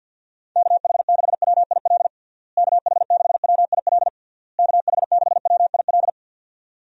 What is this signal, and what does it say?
Morse code, heard as a keyed tone, about 52 wpm, 700 Hz. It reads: KH6WIL KH6WIL KH6WIL
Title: USA Call Signs - 6 Characters - Repeated 3x - 50wpm